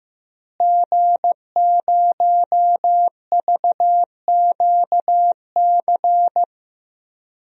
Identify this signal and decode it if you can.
Morse code, heard as a keyed tone, about 15 wpm, 700 Hz. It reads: G0VQC
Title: DXCC-100 Call Signs - 5 Characters - Repeated 2x - 15wpm